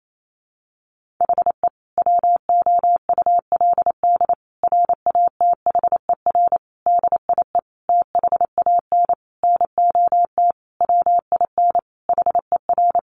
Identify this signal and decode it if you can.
Morse code, heard as a keyed tone, about 28 wpm, 700 Hz. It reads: HE WOULD RATHER DIE THAN NOT WIN HER
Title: Sentences from Top 1500 Words 28wpm